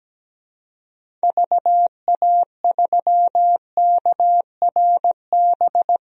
Morse code, heard as a keyed tone, about 17 wpm, 700 Hz. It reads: VA3KRB
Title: DXCC-100 Call Signs - 6 Characters - 17wpm